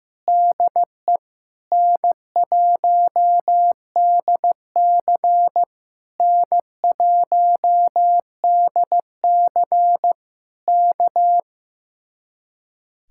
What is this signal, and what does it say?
Morse code, heard as a keyed tone, about 15 wpm, 700 Hz. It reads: DE N1DC N1DC K
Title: Speed-Racing Common QSO Phrases Encore - 15wpm